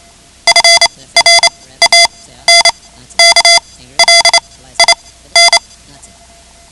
Morse code, heard as a keyed tone, about 28 wpm, spaced as for 15 wpm, 2050 Hz. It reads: FRANKLIN
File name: ham radio / CodeQuick / Code Quick 10